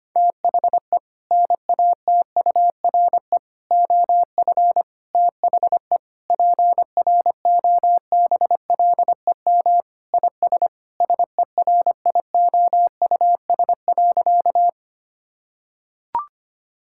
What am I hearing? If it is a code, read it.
Morse code, heard as a keyed tone, about 25 wpm, 700 Hz. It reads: THE NATURE OF THE PROBLEM IS SERIOUS.
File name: practice / 25wpm / Sentences with 3-Word Phrases - 25wpm